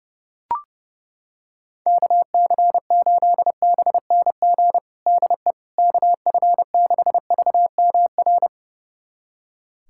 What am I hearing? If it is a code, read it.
Morse code, heard as a keyed tone, about 30 wpm, 700 Hz. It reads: KC8BNG DE KF6VMR
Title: USA Callsign DE Callsign - 6 Characters - 30wpm